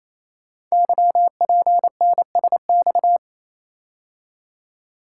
Morse code, heard as a keyed tone, about 27 wpm, 700 Hz. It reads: YPNSX
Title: Code Groups - 5 Letter:Numbers - 28wpm